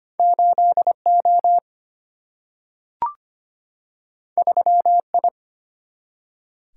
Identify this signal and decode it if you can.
Morse code, heard as a keyed tone, about 25 wpm, 700 Hz. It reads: 8O 3I
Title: Single-Letter Single-Number 25wpm